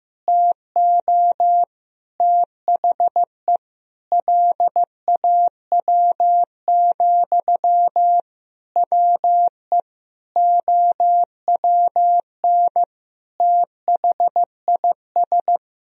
Morse code, heard as a keyed tone, about 15 wpm, 700 Hz. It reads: TO THE LAW, WE OWN THIS
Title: Sentences from Top 1000 Words - Review Part 4 - 15wpm